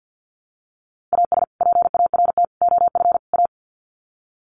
Morse code, heard as a keyed tone, about 51 wpm, 700 Hz. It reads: AS PART OF A
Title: Sets of 4 Words 50wpm